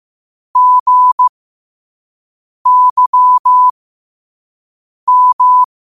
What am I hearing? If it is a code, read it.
Morse code, heard as a keyed tone, about 15 wpm, spaced as for 5 wpm, 1000 Hz. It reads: GYM
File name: qotdfiles / 5wpm